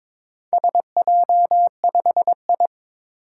Morse code, heard as a keyed tone, about 22 wpm, 700 Hz. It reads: SJ5I
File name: practice / 22wpm / Four Letter-Number Combinations - 22wpm